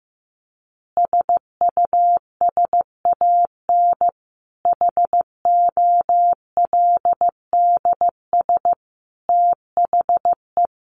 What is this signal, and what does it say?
Morse code, heard as a keyed tone, about 15 wpm, 700 Hz. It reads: SUSAN HOLDS THE
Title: Easy Phrases 15wpm